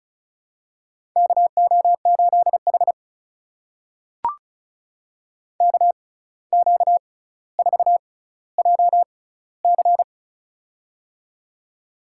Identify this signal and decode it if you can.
Morse code, heard as a keyed tone, about 36 wpm, spaced as for 17 wpm, 700 Hz. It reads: KO8H KQ4JC
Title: USA Call Signs - 6x Character Spacing - 35wpm